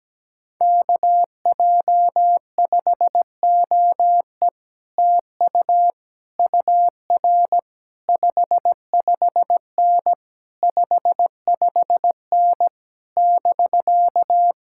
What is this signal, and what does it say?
Morse code, heard as a keyed tone, about 17 wpm, 700 Hz. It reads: KJ5OE TU UR 55N 55N <BK>
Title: Speed-Racing Common QSO Phrases Encore - 17wpm